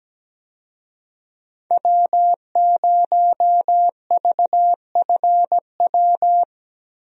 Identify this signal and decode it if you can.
Morse code, heard as a keyed tone, about 17 wpm, 700 Hz. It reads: W0VFW
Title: USA Call Signs - 5 Characters - Repeated 2x - 17wpm